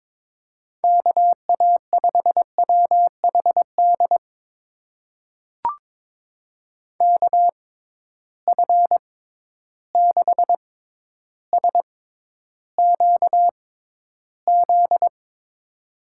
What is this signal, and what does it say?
Morse code, heard as a keyed tone, about 22 wpm, spaced as for 11 wpm, 700 Hz. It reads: KA5WHD KF6SQZ
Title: USA Call Signs - 6 Characters - 6x Character Spacing - 22wpm